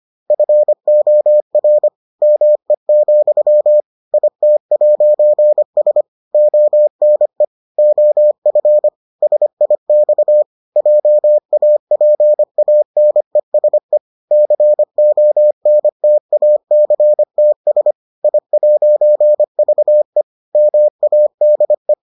Morse code, heard as a keyed tone, about 25 wpm, 600 Hz. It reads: FOR ME, IT'S ONE OF SIX JAPANESE CONTACTS I'VE MADE